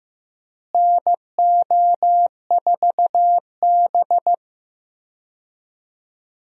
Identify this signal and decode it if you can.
Morse code, heard as a keyed tone, about 15 wpm, 700 Hz. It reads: NO4B
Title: USA Call Signs - 4 Characters - Repeated 2x - 15wpm